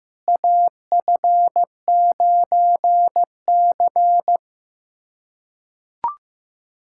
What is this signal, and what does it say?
Morse code, heard as a keyed tone, about 16 wpm, 700 Hz. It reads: AF9C
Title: USA Callsign DE Callsign - 4 Characters - 15wpm